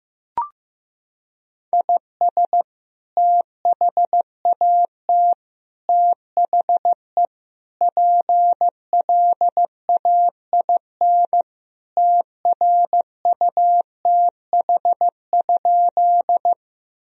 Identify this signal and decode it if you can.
Morse code, heard as a keyed tone, about 15 wpm, 700 Hz. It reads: IS THAT THE PLAIN TRUTH?
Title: Sentences from Top 1000 Words - Review Part 2 - 15wpm